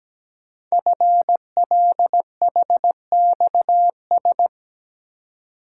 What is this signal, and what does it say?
Morse code, heard as a keyed tone, about 17 wpm, 700 Hz. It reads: FLHXS
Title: Code Groups - 5 Letters - 17wpm